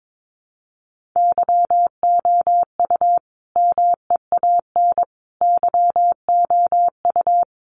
Morse code, heard as a keyed tone, about 22 wpm, 700 Hz. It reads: YOU MEAN YOU
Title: Sentences from Top 1400 Words 22wpm